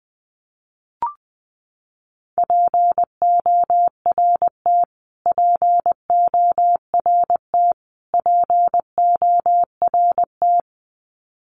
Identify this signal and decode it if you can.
Morse code, heard as a keyed tone, about 20 wpm, 700 Hz. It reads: PORT PORT PORT
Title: Top 200 Words Repeated 20wpm